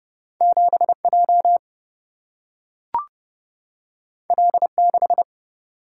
Morse code, heard as a keyed tone, about 30 wpm, 700 Hz. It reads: 7J L6
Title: Single-Letter Single-Number 30wpm